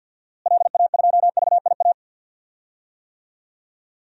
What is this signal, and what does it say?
Morse code, heard as a keyed tone, about 47 wpm, 700 Hz. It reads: LA2VIA
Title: DXCC-100 Call Signs - 6 Characters - Repeated 3x - 50wpm